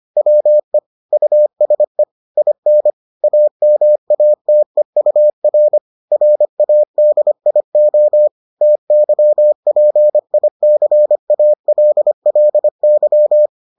Morse code, heard as a keyed tone, about 25 wpm, 600 Hz. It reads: WE USE IN AMATEUR RADIO TYPICALLY